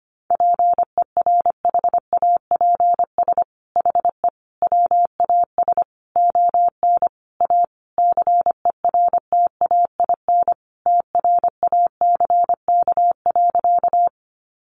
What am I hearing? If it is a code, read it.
Morse code, heard as a keyed tone, about 25 wpm, 700 Hz. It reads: PERHAPS HE WAS ON A CERTAIN TRACK.
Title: Sentences from Top 1100 Words 25wpm